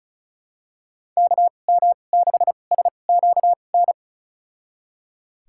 Morse code, heard as a keyed tone, about 36 wpm, spaced as for 25 wpm, 700 Hz. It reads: KM6SQN
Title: USA Call Signs - 6 Characters - 2x Character Spacing - 35wpm